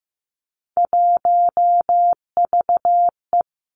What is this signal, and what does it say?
Morse code, heard as a keyed tone, about 15 wpm, 700 Hz. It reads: 1VE
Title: Three Challenging Letter-Number Combinations 15wpm